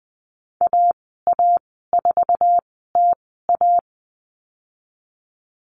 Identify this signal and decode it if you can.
Morse code, heard as a keyed tone, about 20 wpm, spaced as for 14 wpm, 700 Hz. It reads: AA4TA
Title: 048 Lesson 9g - Number 4 - Call Signs 20wpm